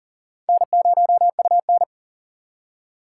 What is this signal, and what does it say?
Morse code, heard as a keyed tone, about 40 wpm, 700 Hz. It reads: N0UN
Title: USA Call Signs - 3x Character Spacing - 40wpm